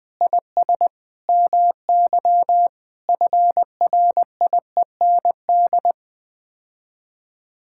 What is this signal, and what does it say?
Morse code, heard as a keyed tone, about 20 wpm, 700 Hz. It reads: IS MY FRIEND
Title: Sets of 8 Words - 20wpm